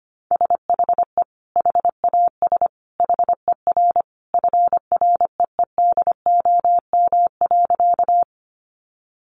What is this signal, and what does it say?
Morse code, heard as a keyed tone, about 25 wpm, 700 Hz. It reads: SHE HAS HER FREEDOM.
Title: Sentences from Top 800 Words 25wpm